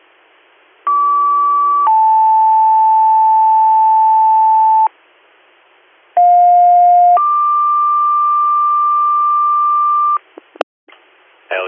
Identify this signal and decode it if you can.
Two-tone page, heard as a one-second tone, then a three-second tone, about 1150 then 880 Hz, about 710 then 1180 Hz. It is heard from 0.9 to 10.2 s.